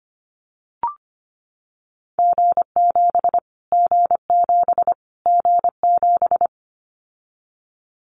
Morse code, heard as a keyed tone, about 25 wpm, 700 Hz. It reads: G7 G7 G7
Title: Callsign Prefixes for Top 25 DXCC - Repeated 3x - 25wpm